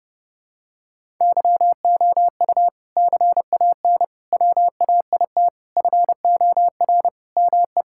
Morse code, heard as a keyed tone, about 30 wpm, 700 Hz. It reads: YOU CAN WAIT FOR ME
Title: Sentences from Top 1000 Words - Review Part 6 - 30wpm